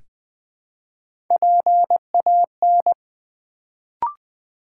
Morse code, heard as a keyed tone, about 20 wpm, 700 Hz. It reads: PAN
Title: Build A Word - 1 Letter At A Time - 3 to 5-Letter Words - 20wpm